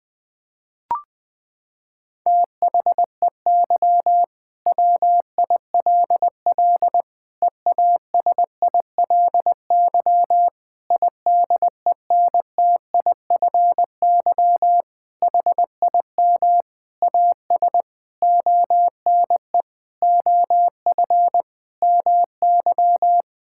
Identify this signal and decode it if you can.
Morse code, heard as a keyed tone, about 20 wpm, 700 Hz. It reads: THEY WILL EASILY IDENTIFY HIM AS ONE OF MY
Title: Sentences from Top 1000 Words - Review Part 6 - 20wpm